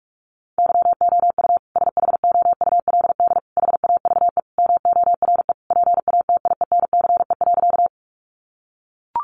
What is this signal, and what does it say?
Morse code, heard as a keyed tone, about 44 wpm, 700 Hz. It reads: YOU SHOULD HAVE MORE PATIENCE.
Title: Sentences from Top 1300 Words 45wpm